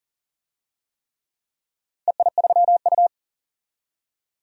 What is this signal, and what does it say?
Morse code, heard as a keyed tone, about 40 wpm, 700 Hz. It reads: EI3U
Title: DXCC-100 Call Signs - 4 Characters - 40wpm